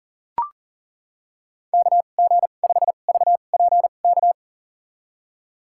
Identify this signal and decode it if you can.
Morse code, heard as a keyed tone, about 40 wpm, spaced as for 29 wpm, 700 Hz. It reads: KG5VPK
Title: USA Call Signs - 6 Characters - 2x Character Spacing - 40wpm